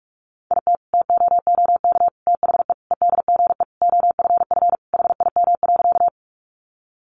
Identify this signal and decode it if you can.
Morse code, heard as a keyed tone, about 44 wpm, 700 Hz. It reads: IT TOOK THE EDGE OFF HIM.
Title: Sentences from Top 1300 Words 45wpm